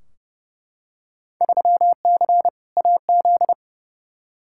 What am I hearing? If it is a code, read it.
Morse code, heard as a keyed tone, about 30 wpm, 700 Hz. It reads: 3C AZ
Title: Field Day Exchanges Encore - 30wpm